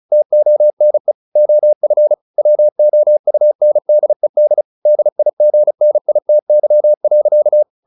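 Morse code, heard as a keyed tone, about 34 wpm, 600 Hz. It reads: TONE OF WOUNDED DIGNITY.
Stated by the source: Morse practice at 35 wpm from CW med Gnister, sending Ken Follett's novel Never in English.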